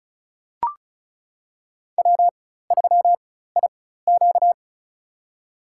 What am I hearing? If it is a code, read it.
Morse code, heard as a keyed tone, about 36 wpm, spaced as for 16 wpm, 700 Hz. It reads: W3IQ
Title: USA Call Signs - 4x Character Spacing - 35wpm